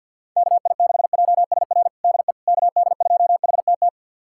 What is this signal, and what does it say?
Morse code, heard as a keyed tone, about 53 wpm, 700 Hz. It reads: KI6JSR DE KD2HTT
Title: USA Callsign DE Callsign - 6 Characters - 50wpm